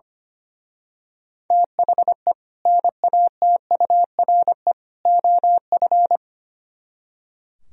Morse code, heard as a keyed tone, about 28 wpm, 700 Hz. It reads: THE NATURE OF
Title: Speed-Racing - Sets of 3 Words 25wpm